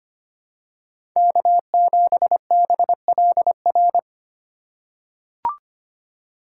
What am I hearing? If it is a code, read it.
Morse code, heard as a keyed tone, about 25 wpm, 700 Hz. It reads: K7BLR
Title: USA Call Signs - 5 Characters - 4x Character Spacing - 25wpm